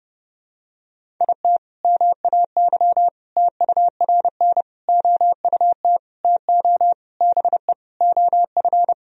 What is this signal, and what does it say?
Morse code, heard as a keyed tone, about 30 wpm, 700 Hz. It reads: IT MAY TURN OUT TO BE OF
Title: Sentences from Top 400 Words and Abbreviations 30wpm